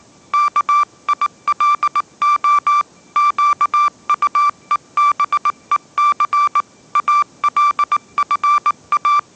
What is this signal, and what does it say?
Morse code, heard as a keyed tone, about 19 wpm, 1170 Hz. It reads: KILO QUEBEC ALFA
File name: mrx083 20wpm